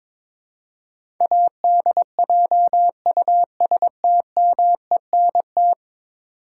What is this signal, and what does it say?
Morse code, heard as a keyed tone, about 21 wpm, 700 Hz. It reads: ADJUSTMENT
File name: practice / 22wpm / 10 Letter Words - First 3 Letters Given - 22wpm